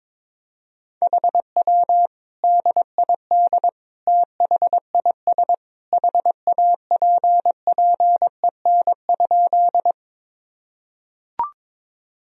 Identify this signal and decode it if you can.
Morse code, heard as a keyed tone, about 22 wpm, 700 Hz. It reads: HW DID THIS HAPPEN?
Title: Sentences from Top 400 Words and Abbreviations 22wpm